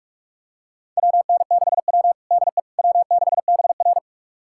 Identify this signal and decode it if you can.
Morse code, heard as a keyed tone, about 46 wpm, 700 Hz. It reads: WN6W DE W6BR
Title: USA Callsign DE Callsign - 4 Characters - 45wpm